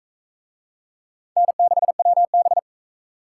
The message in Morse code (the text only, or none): N6WB